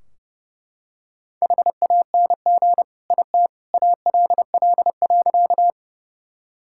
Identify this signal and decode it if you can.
Morse code, heard as a keyed tone, about 32 wpm, 700 Hz. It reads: HANG IT ALL.
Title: Sentences from Top 1600 Words 30wpm